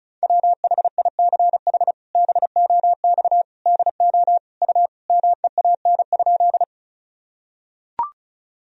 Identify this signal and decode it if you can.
Morse code, heard as a keyed tone, about 36 wpm, 700 Hz. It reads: WHICH BOX DO U MEAN?
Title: Sentences from Top 500 Words and Abbreviations 35wpm